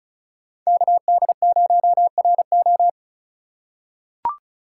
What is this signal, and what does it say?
Morse code, heard as a keyed tone, about 36 wpm, 700 Hz. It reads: KD0RO
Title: USA Call Signs - 5 Characters - 4x Character Spacing - 35wpm